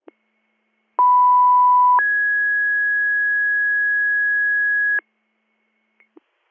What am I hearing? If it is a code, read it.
Two-tone page, heard as a one-second tone, then a three-second tone, about 990 then 1650 Hz. It is heard from 1.0 to 5.0 s.